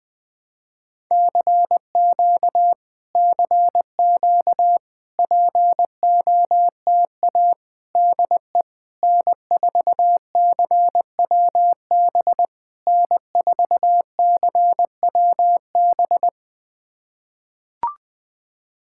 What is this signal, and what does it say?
Morse code, heard as a keyed tone, about 20 wpm, 690 Hz. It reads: CQ CQ POTA DE N4CWB N4CWB
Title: POTA QSO Practice - 20wpm